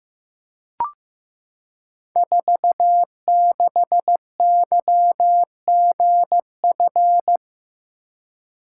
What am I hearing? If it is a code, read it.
Morse code, heard as a keyed tone, about 15 wpm, 700 Hz. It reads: 46YGF
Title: Code Groups - 5 Letter:Numbers - 15wpm